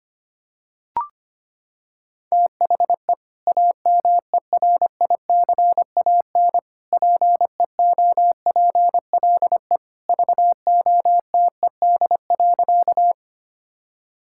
Morse code, heard as a keyed tone, about 25 wpm, 700 Hz. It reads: THE AMERICAN PEOPLE VOTED.
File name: practice / 25wpm / Sentences with 3-Word Phrases - 25wpm